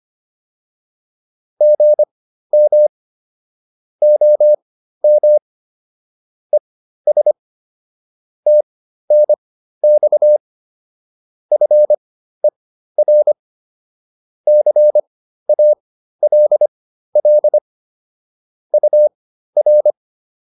Morse code, heard as a keyed tone, about 25 wpm, spaced as for 13 wpm, 600 Hz. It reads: GM OM ES TNX FER CALL UR
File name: QSO208 13